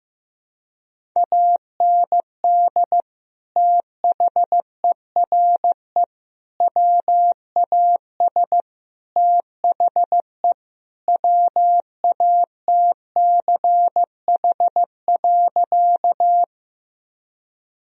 Morse code, heard as a keyed tone, about 15 wpm, 700 Hz. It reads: AND THERE WAS THE WATCH.
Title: Sentences from Top 1000 Words - Review Part 1 - 15wpm